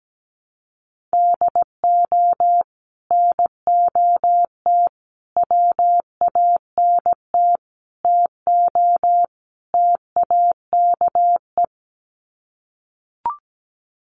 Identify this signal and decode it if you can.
Morse code, heard as a keyed tone, about 17 wpm, 700 Hz. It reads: DO NOT WANT TO TAKE
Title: Sets of 5 Words Encore2 17wpm